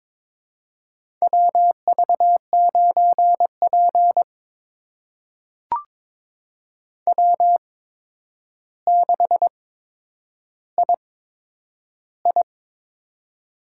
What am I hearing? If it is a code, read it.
Morse code, heard as a keyed tone, about 22 wpm, spaced as for 9 wpm, 700 Hz. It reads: WV9P W6II
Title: USA Call Signs - 4 Characters - 8x Character Spacing - 22wpm